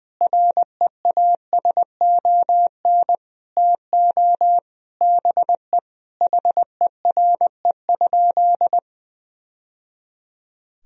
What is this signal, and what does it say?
Morse code, heard as a keyed tone, about 20 wpm, 700 Hz. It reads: REASON TO BE HERE?
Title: Sentences from Top 1000 Words - Review Part 4 - 20wpm